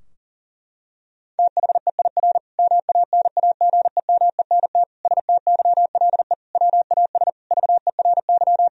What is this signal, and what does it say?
Morse code, heard as a keyed tone, about 40 wpm, 700 Hz. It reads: THEIR MANAGEMENT STYLE WAS VERY
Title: Sentences from Top 1000 Words - Review Part 4 - 40wpm